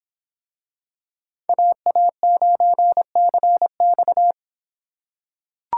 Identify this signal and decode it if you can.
Morse code, heard as a keyed tone, about 25 wpm, 700 Hz. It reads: AA9CX